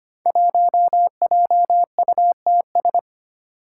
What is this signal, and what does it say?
Morse code, heard as a keyed tone, about 25 wpm, 700 Hz. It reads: 1JUTS
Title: Five Challenging Letter-Number Combinations 25wpm